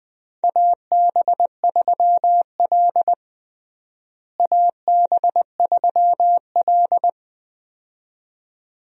Morse code, heard as a keyed tone, about 20 wpm, 700 Hz. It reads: AB3L AB3L
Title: USA Call Signs - 4 Characters - Repeated 2x - 20wpm